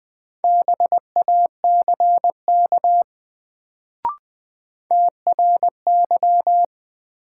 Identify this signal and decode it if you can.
Morse code, heard as a keyed tone, about 20 wpm, 700 Hz. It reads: BACK TRY
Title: Sets of 2 Words - New Word Drill - 20wpm